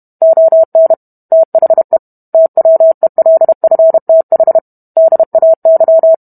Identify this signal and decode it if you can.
Morse code, heard as a keyed tone, about 32 wpm, 650 Hz. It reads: ON THE TWELFTH DAY